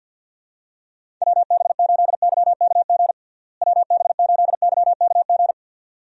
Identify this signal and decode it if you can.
Morse code, heard as a keyed tone, about 49 wpm, 700 Hz. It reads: WB8YKG WB8YKG
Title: USA Call Signs - 6 Characters - Repeated 2x - 50wpm